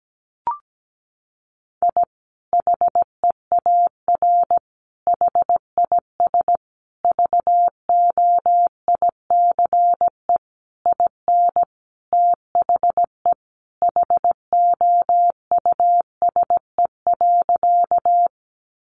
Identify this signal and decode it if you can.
Morse code, heard as a keyed tone, about 17 wpm, 700 Hz. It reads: I HEAR HIS VOICE IN THE HOUSE.
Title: Sentences from Top 500 Words 17wpm